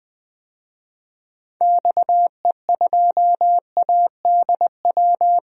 Morse code, heard as a keyed tone, about 20 wpm, 700 Hz. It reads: XE2ADW
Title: DXCC-100 Call Signs - 6 Characters - Repeated 3x - 20wpm